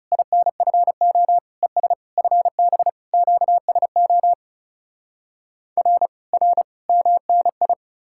Morse code, heard as a keyed tone, about 40 wpm, 700 Hz. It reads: INFO ES FB QSO R R MNI